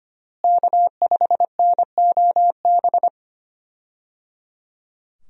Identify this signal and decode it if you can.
Morse code, heard as a keyed tone, about 25 wpm, 700 Hz. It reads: K5NOB
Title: USA Call Signs - 5 Characters - Repeated 3x - 25wpm